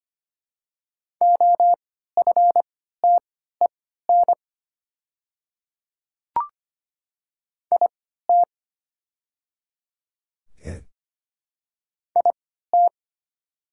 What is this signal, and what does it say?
Morse code, heard as a keyed tone, about 25 wpm, spaced as for 14 wpm, 700 Hz. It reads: OFTEN IT IT
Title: 3x Character Spacing - Top 300 Words - 25wpm